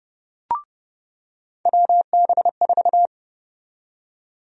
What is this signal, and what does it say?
Morse code, heard as a keyed tone, about 30 wpm, 700 Hz. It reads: WB4
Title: USA Call Sign Prefixes - 30wpm